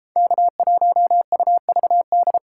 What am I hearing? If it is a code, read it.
Morse code, heard as a keyed tone, about 31 wpm, 700 Hz. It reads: K1UVD